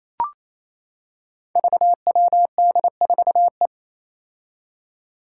Morse code, heard as a keyed tone, about 28 wpm, 700 Hz. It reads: VWD4E
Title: Five Challenging Letter-Number Combinations 28wpm